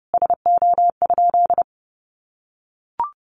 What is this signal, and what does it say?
Morse code, heard as a keyed tone, about 30 wpm, 700 Hz. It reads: SO?
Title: Sentences from Top 1400 Words 30wpm